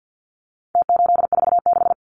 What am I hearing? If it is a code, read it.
Morse code, heard as a keyed tone, about 48 wpm, 700 Hz. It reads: T846
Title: Numbers - Contest Style - 4-Digit 50wpm